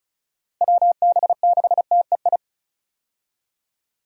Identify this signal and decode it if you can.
Morse code, heard as a keyed tone, about 36 wpm, 700 Hz. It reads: WB6TEI
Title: USA Call Signs - 6 Characters - Repeated 2x - 35wpm